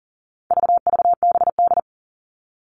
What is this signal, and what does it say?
Morse code, heard as a keyed tone, about 40 wpm, 700 Hz. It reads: VVBD
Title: All Cut Numbers - 4-Digit 40wpm